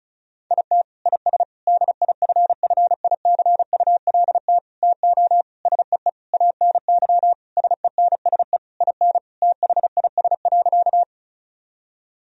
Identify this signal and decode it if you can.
Morse code, heard as a keyed tone, about 34 wpm, 700 Hz. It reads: IT IS DIFFICULT TO SEE ANY SENSE IN THIS.
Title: Sentences from Top 1000 Words - Review Part 4 - 35wpm